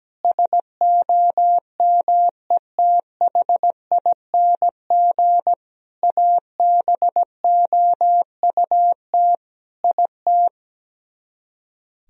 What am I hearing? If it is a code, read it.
Morse code, heard as a keyed tone, about 17 wpm, 700 Hz. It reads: SOMETHING ABOUT IT
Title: Speed-Racing - Sets of 5 Words 17wpm